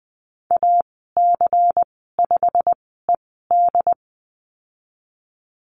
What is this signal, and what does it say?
Morse code, heard as a keyed tone, about 20 wpm, spaced as for 15 wpm, 700 Hz. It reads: AC5ED
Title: 179 Lesson 29g - Slash - Call Signs 20wpm